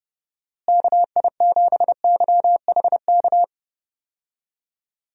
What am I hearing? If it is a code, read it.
Morse code, heard as a keyed tone, about 32 wpm, 700 Hz. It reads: KI7YHK
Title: USA Call Signs - 6 Characters - Repeated 3x - 30wpm